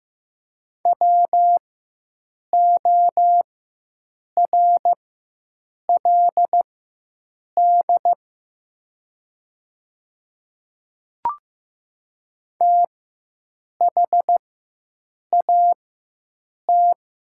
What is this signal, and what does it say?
Morse code, heard as a keyed tone, about 15 wpm, spaced as for 7 wpm, 700 Hz. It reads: WORLD THAT
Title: 4x Character Spacing - Top 200 Words - 15wpm